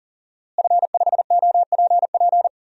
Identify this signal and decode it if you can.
Morse code, heard as a keyed tone, about 40 wpm, 690 Hz. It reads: F5OPP
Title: DXCC-100 Call Signs - 5 Characters - Repeated 2x - 40wpm